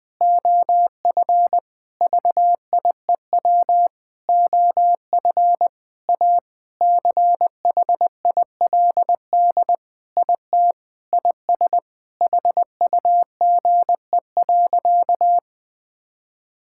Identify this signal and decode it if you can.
Morse code, heard as a keyed tone, about 20 wpm, 700 Hz. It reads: OF VIEW OF A CHILD IT IS HUGE.
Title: Sentences with 5-Word Phrases - 20wpm